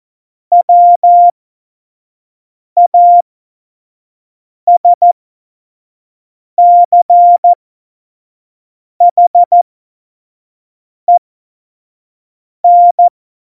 Morse code, heard as a keyed tone, about 14 wpm, spaced as for 5 wpm, 700 Hz. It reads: WASCHEN
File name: Morsekurs / Witze / Witz 28 Juli 2020, 25 F70